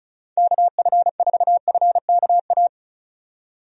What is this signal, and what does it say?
Morse code, heard as a keyed tone, about 34 wpm, 700 Hz. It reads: KF4FKA
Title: USA Call Signs - 6 Characters - 35wpm